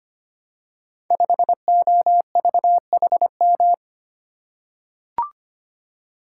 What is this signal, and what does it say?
Morse code, heard as a keyed tone, about 25 wpm, 700 Hz. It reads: 5OVHM
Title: Code Groups - 5 Letter:Numbers - 25wpm